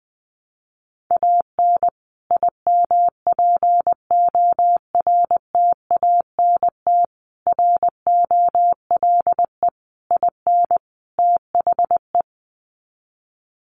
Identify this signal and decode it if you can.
Morse code, heard as a keyed tone, about 20 wpm, 700 Hz. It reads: AN IMPORTANT ROLE IN THE
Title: Sets of 5 Words Encore2 20wpm